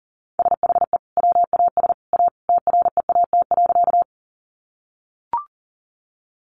Morse code, heard as a keyed tone, about 37 wpm, 700 Hz. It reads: SHE WAS A TREAT.
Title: Sentences from Top 1300 Words 40wpm